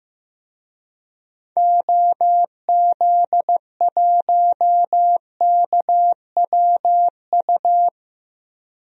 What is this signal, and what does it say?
Morse code, heard as a keyed tone, about 15 wpm, 700 Hz. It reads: OZ1KWU
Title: DXCC-100 Call Signs - 6 Characters - 15wpm